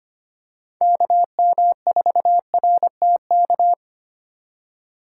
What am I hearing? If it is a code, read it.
Morse code, heard as a keyed tone, about 25 wpm, 700 Hz. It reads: KM4RTK
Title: USA Call Signs - 6 Characters - 6x Character Spacing - 25wpm